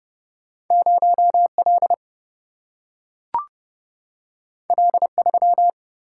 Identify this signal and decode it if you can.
Morse code, heard as a keyed tone, about 30 wpm, 700 Hz. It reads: 0L L3
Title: Single-Letter Single-Number 30wpm